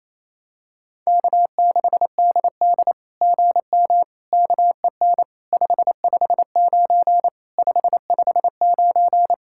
Morse code, heard as a keyed tone, about 28 wpm, 700 Hz. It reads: K6DD GM KEN 559 559
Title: SOTA QSO Practice 28wpm